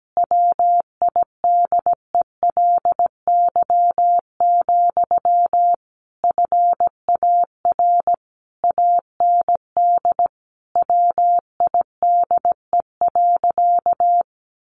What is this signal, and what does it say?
Morse code, heard as a keyed tone, about 17 wpm, 700 Hz. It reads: WIDELY, FAR AND WIDE.
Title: Sentences from Top 1200 Words 17wpm